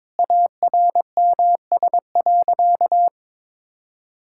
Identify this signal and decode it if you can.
Morse code, heard as a keyed tone, about 22 wpm, 700 Hz. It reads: ARMS.
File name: practice / 22wpm / Easy Sentences with Hints - 22wpm